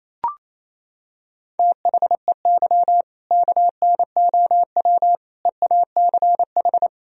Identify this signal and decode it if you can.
Morse code, heard as a keyed tone, about 28 wpm, 700 Hz. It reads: THEY KNOW EACH
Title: Sentences from Top 1000 Words - Review Part 1 - 28wpm